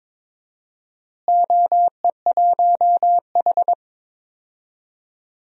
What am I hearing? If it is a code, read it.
Morse code, heard as a keyed tone, about 22 wpm, 700 Hz. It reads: OE1H
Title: DXCC-100 Call Signs - 4 Characters - Repeated 3x - 22wpm